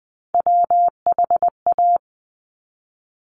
Morse code, heard as a keyed tone, about 20 wpm, 700 Hz. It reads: WHA
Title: Top Three Letter Combinations in Words 20wpm